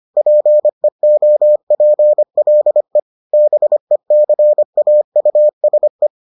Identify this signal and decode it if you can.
Morse code, heard as a keyed tone, about 25 wpm, 600 Hz. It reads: PEOPLE BECAUSE